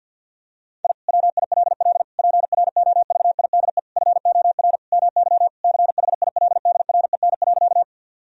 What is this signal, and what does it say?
Morse code, heard as a keyed tone, about 50 wpm, 700 Hz. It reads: I WILL PROVIDE FOR MY CHILDREN.
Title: Sentences from Top 1000 Words - Review Part 3 - 50wpm